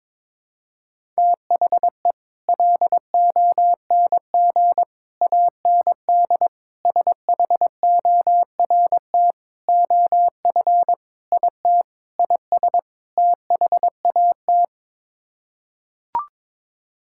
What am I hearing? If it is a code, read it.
Morse code, heard as a keyed tone, about 22 wpm, 700 Hz. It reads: THE LONG AND SHORT OF IT IS THAT
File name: practice / 22wpm / Sets of 8 Words - 22wpm